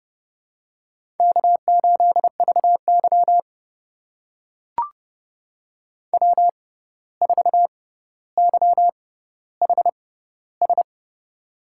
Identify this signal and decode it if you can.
Morse code, heard as a keyed tone, about 31 wpm, spaced as for 14 wpm, 700 Hz. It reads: K8VY W4YHS
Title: USA Call Signs - 6x Character Spacing - 30wpm